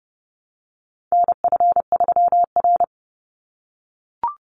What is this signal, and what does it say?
Morse code, heard as a keyed tone, about 30 wpm, 700 Hz. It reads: NF3R
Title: Call Signs - 30wpm